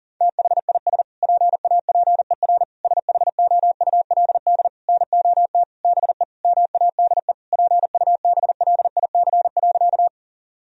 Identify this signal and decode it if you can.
Morse code, heard as a keyed tone, about 40 wpm, 700 Hz. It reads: THIS PAPER SHOULD NOT BE MADE PUBLIC.
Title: Sentences from Top 1000 Words - Review Part 3 - 40wpm